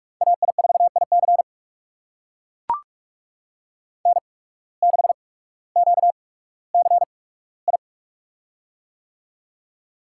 Morse code, heard as a keyed tone, about 44 wpm, spaced as for 19 wpm, 700 Hz. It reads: AI4IC N6QCI
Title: USA Call Signs - 5 Characters - 8x Character Spacing - 45wpm